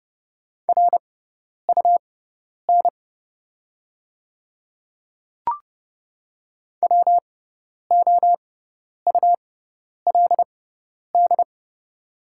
Morse code, heard as a keyed tone, about 30 wpm, spaced as for 10 wpm, 690 Hz. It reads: RUN WOULD
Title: 6x Character Spacing - Top 500 Words - 30wpm